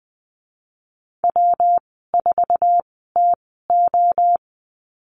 Morse code, heard as a keyed tone, about 19 wpm, spaced as for 14 wpm, 700 Hz. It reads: W4TO
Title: 173 Lesson 28g - Number 9 - Call Signs 20wpm